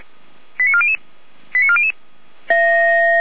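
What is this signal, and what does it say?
Five-tone selective call (ZVEI), heard as quick runs of five tones, tones of about 70 ms. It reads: 89400 89400
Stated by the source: ZVEI five-tone selective call received on an SDR.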